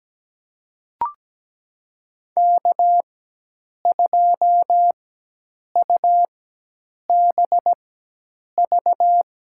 Morse code, heard as a keyed tone, about 17 wpm, spaced as for 8 wpm, 700 Hz. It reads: K2UBV
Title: USA Call Signs - 5 Characters - 4x Character Spacing - 17wpm